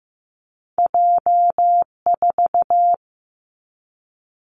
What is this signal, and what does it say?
Morse code, heard as a keyed tone, about 15 wpm, 700 Hz. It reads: J 4
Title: Two Challenging Letter-Number Combinations 15wpm